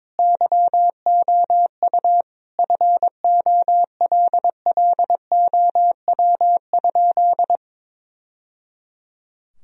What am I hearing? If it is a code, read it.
Morse code, heard as a keyed tone, about 22 wpm, 700 Hz. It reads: YOU FOLLOW?
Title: Sentences from Top 1000 Words - Review Part 3 - 22wpm